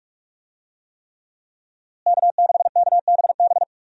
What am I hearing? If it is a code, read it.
Morse code, heard as a keyed tone, about 45 wpm, 700 Hz. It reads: K6KBB
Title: USA Call Signs - 5 Characters - Repeated 3x - 45wpm